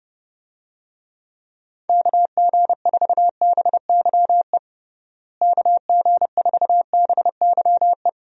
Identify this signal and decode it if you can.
Morse code, heard as a keyed tone, about 30 wpm, 700 Hz. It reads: KG4BYE KG4BYE
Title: USA Call Signs - 6 Characters - Repeated 3x - 30wpm